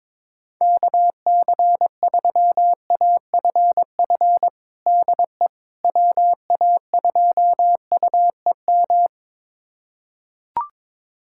Morse code, heard as a keyed tone, about 22 wpm, 700 Hz. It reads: KC3AFF DE WA2UEM
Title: USA Callsign DE Callsign - 6 Characters - 22wpm